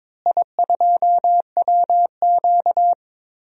Morse code, heard as a keyed tone, about 22 wpm, 700 Hz. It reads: I2WQ
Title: DXCC-100 Call Signs - 4 Characters - Repeated 3x - 22wpm